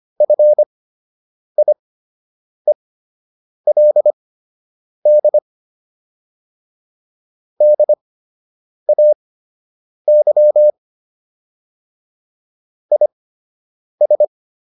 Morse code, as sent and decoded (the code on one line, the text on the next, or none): ..-. .. .-.. -.. / -.. .- -.-- / .. ...
FIELD DAY IS